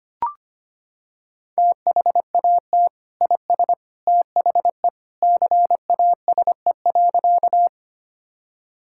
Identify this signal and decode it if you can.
Morse code, heard as a keyed tone, about 25 wpm, 700 Hz. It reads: THAT IS THE CASE.
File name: practice / 25wpm / Sentences with 2-Word Phrases - 25wpm